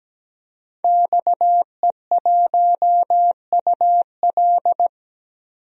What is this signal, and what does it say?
Morse code, heard as a keyed tone, about 17 wpm, 700 Hz. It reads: XE1UL
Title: DXCC-100 Call Signs - 5 Characters - 17wpm